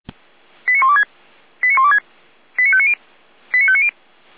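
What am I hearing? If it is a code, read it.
Five-tone selective call (ZVEI), heard as quick runs of five tones, tones of about 70 ms. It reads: 89126 89126 89590 89590
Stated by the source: ZVEI five-tone selective call received via SDR.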